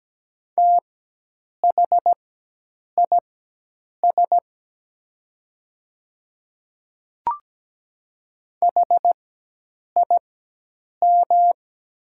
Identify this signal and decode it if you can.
Morse code, heard as a keyed tone, about 17 wpm, spaced as for 8 wpm, 700 Hz. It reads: THIS HIM
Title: 4x Character Spacing - Top 100 Words - 17wpm